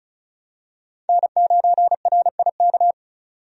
Morse code, heard as a keyed tone, about 34 wpm, 700 Hz. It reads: N9RIK
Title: USA Call Signs - 5 Characters - 35wpm